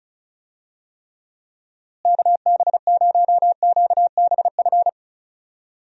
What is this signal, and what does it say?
Morse code, heard as a keyed tone, about 34 wpm, 700 Hz. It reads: KB0QBF
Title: USA Call Signs - 6 Characters - Repeated 2x - 35wpm